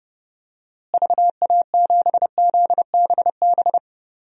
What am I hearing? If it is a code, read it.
Morse code, heard as a keyed tone, about 30 wpm, 700 Hz. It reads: VA7ZBB
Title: DXCC-100 Call Signs - 6 Characters - Repeated 2x - 30wpm